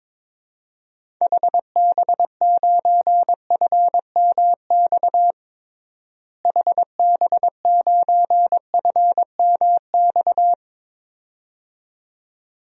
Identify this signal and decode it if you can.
Morse code, heard as a keyed tone, about 22 wpm, 700 Hz. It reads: HB9FMX HB9FMX
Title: DXCC-100 Call Signs - 6 Characters - Repeated 2x - 22wpm